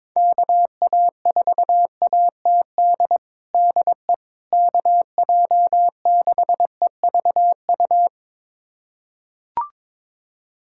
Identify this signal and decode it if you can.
Morse code, heard as a keyed tone, about 22 wpm, 700 Hz. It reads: KA4ATD DE KJ6EVU
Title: USA Callsign DE Callsign - 6 Characters - 22wpm